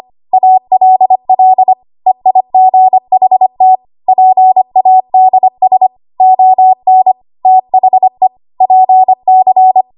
Morse code, heard as a keyed tone, about 25 wpm, 750 Hz. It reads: ALL EIGHT PADS ON THE PC
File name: Archive / 25 WPM / 25WPM